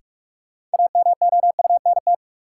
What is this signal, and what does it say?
Morse code, heard as a keyed tone, about 45 wpm, 700 Hz. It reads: AMOUNT
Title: Sentences from Top 700 Words - New Word Drill - 45wpm